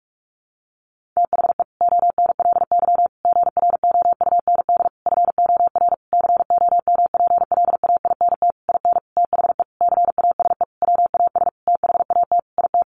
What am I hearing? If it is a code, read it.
Morse code, heard as a keyed tone, about 44 wpm, 700 Hz. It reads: THE ONLY GROUND FOR COMPLAINT IN THE CASE WAS THAT IT
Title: Sentences from Top 1400 Words 45wpm